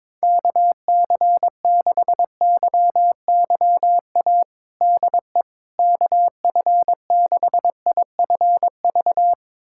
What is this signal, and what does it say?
Morse code, heard as a keyed tone, about 22 wpm, 700 Hz. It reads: KC6YYA DE KF6IFV
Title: USA Callsign DE Callsign - 6 Characters - 22wpm